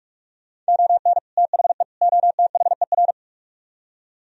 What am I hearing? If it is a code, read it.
Morse code, heard as a keyed tone, about 66 wpm, spaced as for 55 wpm, 700 Hz. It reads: ON THE OTHER
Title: Speed-Racing - Sets of 3 Words 45wpm